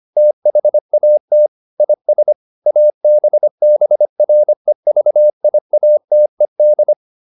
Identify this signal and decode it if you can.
Morse code, heard as a keyed tone, about 25 wpm, 600 Hz. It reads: THAT IS ABBREVIATED